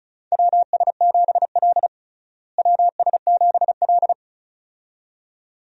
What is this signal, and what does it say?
Morse code, heard as a keyed tone, about 35 wpm, 700 Hz. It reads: WS7L WS7L
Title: USA Call Signs - 4 Characters - Repeated 2x - 35wpm